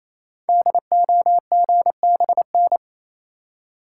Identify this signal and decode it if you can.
Morse code, heard as a keyed tone, about 27 wpm, 700 Hz. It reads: DOGBN
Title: Code Groups - 5 Letters - 28wpm